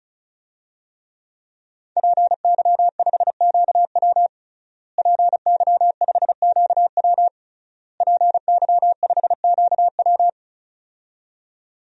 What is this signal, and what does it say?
Morse code, heard as a keyed tone, about 36 wpm, 700 Hz. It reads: PY5QW PY5QW PY5QW
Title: DXCC-100 Call Signs - 5 Characters - Repeated 3x - 35wpm